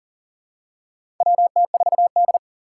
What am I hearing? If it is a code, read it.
Morse code, heard as a keyed tone, about 40 wpm, 700 Hz. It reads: WT4D
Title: USA Call Signs - 4 Characters - 3x Character Spacing - 40wpm